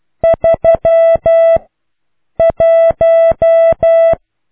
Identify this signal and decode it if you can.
Morse code, heard as a keyed tone, about 12 wpm, 660 Hz. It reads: 3 1